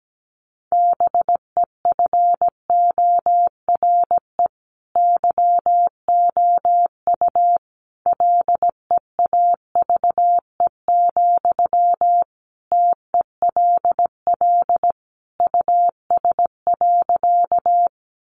Morse code, heard as a keyed tone, about 17 wpm, 700 Hz. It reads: BEFORE YOU LEAVE, TELL US.
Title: Sentences from Top 400 Words 17wpm